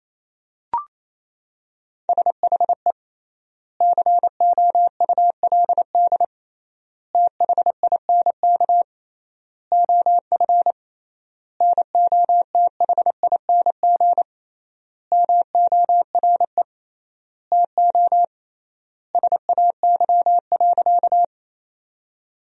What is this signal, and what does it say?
Morse code, heard as a keyed tone, about 27 wpm, 700 Hz. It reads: SHE COULD THINK OF NOTHING MORE TO SAY.
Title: 3x Word-Spacing - Sentences from Top 500 Words - 28wpm